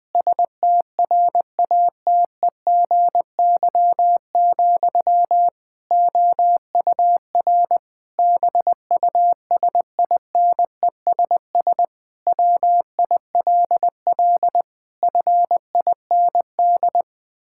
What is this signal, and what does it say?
Morse code, heard as a keyed tone, about 20 wpm, 700 Hz. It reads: STRATEGY, OUR BUSINESS WILL FIND
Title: Sentences from Top 1000 Words - Review Part 5 - 20wpm